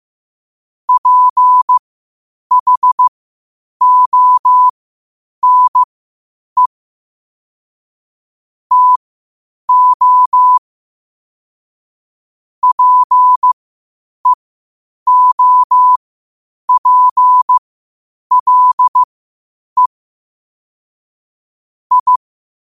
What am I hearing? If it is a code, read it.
Morse code, heard as a keyed tone, about 15 wpm, spaced as for 8 wpm, 1000 Hz. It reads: PHONE TO PEOPLE I
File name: qotdfiles / 8wpm